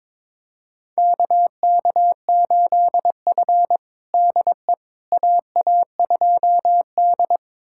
Morse code, heard as a keyed tone, about 22 wpm, 700 Hz. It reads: KK8F DE AA2D
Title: USA Callsign DE Callsign - 4 Characters - 22wpm